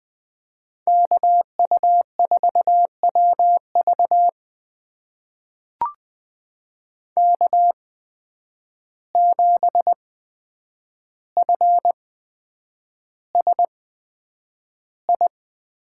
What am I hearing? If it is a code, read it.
Morse code, heard as a keyed tone, about 20 wpm, spaced as for 9 wpm, 700 Hz. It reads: KU4WV K7FSI
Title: USA Call Signs - 5 Characters - 8x Character Spacing - 20wpm